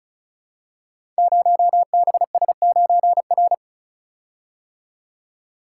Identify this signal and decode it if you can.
Morse code, heard as a keyed tone, about 36 wpm, 690 Hz. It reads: 0BS9R
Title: Code Groups - 5 Letter:Numbers - 35wpm